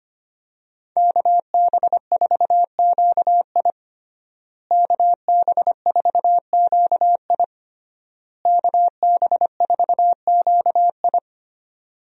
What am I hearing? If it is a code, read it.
Morse code, heard as a keyed tone, about 25 wpm, 700 Hz. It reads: KB4QI KB4QI KB4QI
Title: USA Call Signs - 5 Characters - Repeated 3x - 25wpm